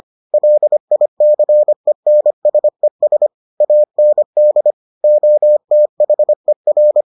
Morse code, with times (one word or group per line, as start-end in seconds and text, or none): 0.3-3.3 LICENSES
3.6-4.7 AND
5.0-7.0 OTHER